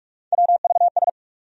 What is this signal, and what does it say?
Morse code, heard as a keyed tone, about 46 wpm, 700 Hz. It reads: WVS
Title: USA Call Sign Postfixes - 3 Letters 45wpm